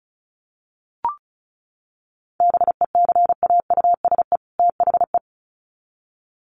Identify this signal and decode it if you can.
Morse code, heard as a keyed tone, about 34 wpm, 700 Hz. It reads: BECAUSE THE
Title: Sets of 2 Words 35wpm